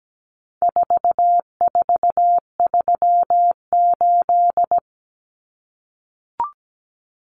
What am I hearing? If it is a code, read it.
Morse code, heard as a keyed tone, about 17 wpm, 700 Hz. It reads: 4438
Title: Numbers - 4-Digit 17wpm